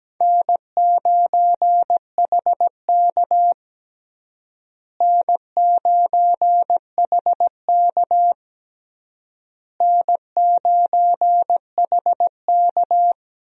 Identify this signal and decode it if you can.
Morse code, heard as a keyed tone, about 17 wpm, 700 Hz. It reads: N9HK N9HK N9HK
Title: USA Call Signs - 4 Characters - Repeated 3x - 17wpm